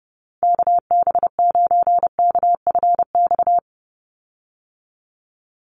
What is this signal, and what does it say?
Morse code, heard as a keyed tone, about 30 wpm, 700 Hz. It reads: KB9KFX
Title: USA Call Signs 30wpm